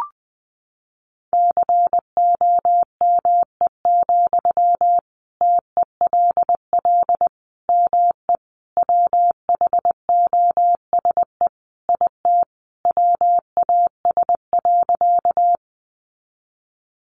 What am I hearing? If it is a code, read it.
Morse code, heard as a keyed tone, about 20 wpm, 700 Hz. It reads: COME, TELL ME WHOSE IT WAS.
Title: Sentences from Top 1200 Words 20wpm